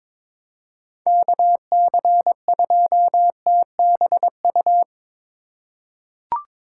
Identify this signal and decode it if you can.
Morse code, heard as a keyed tone, about 21 wpm, 700 Hz. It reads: KC2TBU
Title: USA Call Signs - 6 Characters - 2x Character Spacing - 22wpm